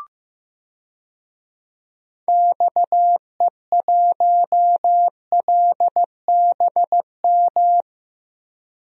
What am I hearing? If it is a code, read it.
Morse code, heard as a keyed tone, about 15 wpm, 700 Hz. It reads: XE1LBM
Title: DXCC-100 Call Signs - 6 Characters - Repeated 2x - 15wpm